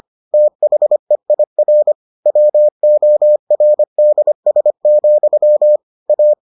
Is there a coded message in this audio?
Yes: Morse code from 0.3 to 6.3 s, heard as a keyed tone, about 25 wpm, 600 Hz. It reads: THEIR WORDS, A